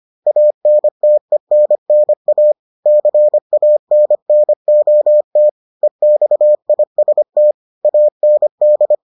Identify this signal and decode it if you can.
Morse code, heard as a keyed tone, about 25 wpm, 600 Hz. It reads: ANTENNA CANNOT EXIST AND